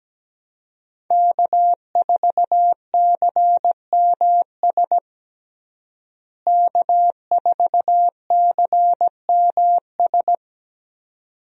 Morse code, heard as a keyed tone, about 17 wpm, 700 Hz. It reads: K4CMS K4CMS
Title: USA Call Signs - 5 Characters - Repeated 2x - 17wpm